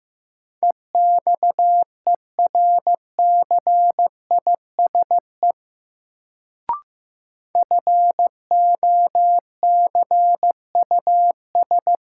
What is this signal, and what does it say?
Morse code, heard as a keyed tone, about 15 wpm, 700 Hz. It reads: EXERCISE FOCUS
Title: Sentences from Top 1000 Words - Review Part 4 - New Word Drill - 15wpm